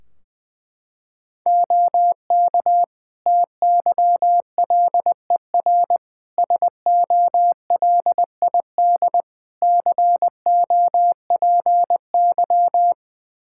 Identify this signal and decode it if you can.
Morse code, heard as a keyed tone, about 20 wpm, 700 Hz. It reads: OK TYLER SOLID COPY
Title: Practice QSO - Exchange 3 - More Info - 20wpm